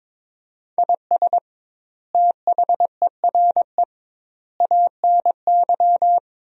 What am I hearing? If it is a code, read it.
Morse code, heard as a keyed tone, about 22 wpm, 700 Hz. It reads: IS THERE ANY
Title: 2x Word-Spacing - Sentences from Top 300 Words - 22wpm